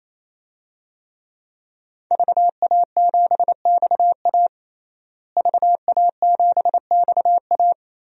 Morse code, heard as a keyed tone, about 28 wpm, 700 Hz. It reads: VA7XA VA7XA
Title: DXCC-100 Call Signs - 5 Characters - Repeated 3x - 28wpm